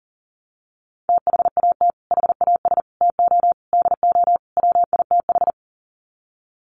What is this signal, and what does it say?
Morse code, heard as a keyed tone, about 40 wpm, 700 Hz. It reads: THAT HAS TO DO WITH
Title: Sets of 5 Words Encore2 40wpm